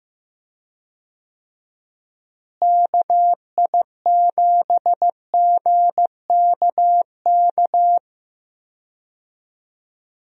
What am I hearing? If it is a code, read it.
Morse code, heard as a keyed tone, about 15 wpm, 700 Hz. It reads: KI7GKK
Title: USA Call Signs - 6 Characters - Repeated 3x - 15wpm